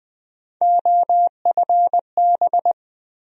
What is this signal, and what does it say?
Morse code, heard as a keyed tone, about 20 wpm, 700 Hz. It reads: OFB
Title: USA Call Sign Postfixes - 3 Letters 20wpm